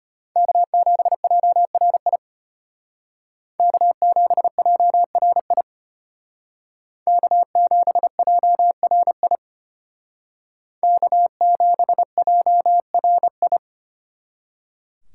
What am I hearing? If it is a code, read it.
Morse code, heard as a keyed tone, about 32 wpm, 700 Hz. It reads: K7JRI K7JRI K7JRI K7JRI